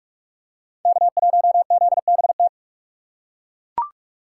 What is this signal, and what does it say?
Morse code, heard as a keyed tone, about 41 wpm, 700 Hz. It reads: K1ZBT
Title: USA Call Signs - 5 Characters - 2x Character Spacing - 45wpm